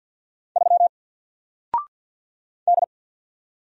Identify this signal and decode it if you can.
Morse code, heard as a keyed tone, about 50 wpm, 700 Hz. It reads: 3 D
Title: Single Character (Letter:Number:Punctuation:Prosign) 50wpm